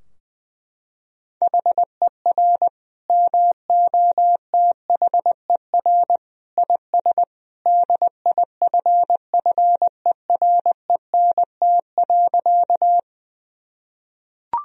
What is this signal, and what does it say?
Morse code, heard as a keyed tone, about 20 wpm, 700 Hz. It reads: HER MOTHER IS DIFFERENT.
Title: Sentences from Top 1000 Words - Review Part 3 - 20wpm